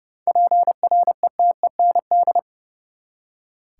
Morse code, heard as a keyed tone, about 30 wpm, 700 Hz. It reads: PRETEND
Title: Seven Letter Words - 30wpm